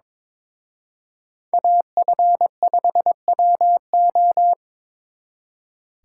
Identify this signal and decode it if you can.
Morse code, heard as a keyed tone, about 26 wpm, 700 Hz. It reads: AF5WO
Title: Speed Racing - USA 2x2 Call Signs 22wpm